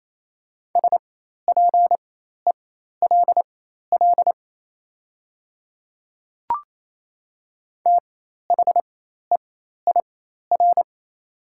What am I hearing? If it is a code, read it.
Morse code, heard as a keyed tone, about 28 wpm, spaced as for 13 wpm, 690 Hz. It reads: SPELL THEIR
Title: 4x Character Spacing - Top 300 Words - 28wpm